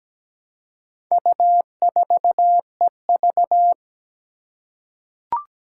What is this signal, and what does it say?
Morse code, heard as a keyed tone, about 17 wpm, 700 Hz. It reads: U4EV
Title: Four Challenging Letter-Number Combinations 17wpm